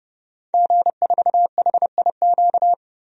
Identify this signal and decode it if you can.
Morse code, heard as a keyed tone, about 30 wpm, 700 Hz. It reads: G4HIQ
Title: DXCC-100 Call Signs - 5 Characters - 30wpm